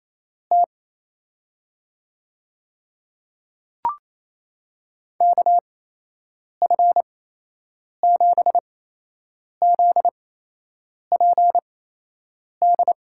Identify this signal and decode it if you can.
Morse code, heard as a keyed tone, about 28 wpm, spaced as for 8 wpm, 700 Hz. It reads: T KF7ZPD
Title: USA Call Signs - 6 Characters - 8x Character Spacing - 28wpm